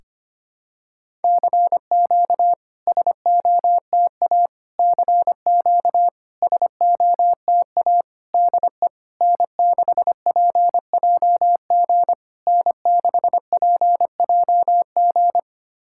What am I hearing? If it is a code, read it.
Morse code, heard as a keyed tone, about 25 wpm, 690 Hz. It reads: CQ SOTA CQ SOTA DE N6PJG N6PJG
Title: SOTA QSO Practice 25wpm